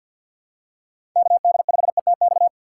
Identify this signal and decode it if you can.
Morse code, heard as a keyed tone, about 52 wpm, 690 Hz. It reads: KD5ETX
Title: USA Call Signs - 6x Character Spacing - 50wpm